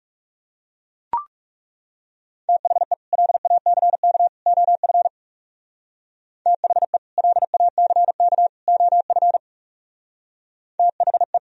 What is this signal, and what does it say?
Morse code, heard as a keyed tone, about 36 wpm, 700 Hz. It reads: THE LACK OF THE LACK OF THE
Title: Speed-Racing - Sets of 3 Words 30wpm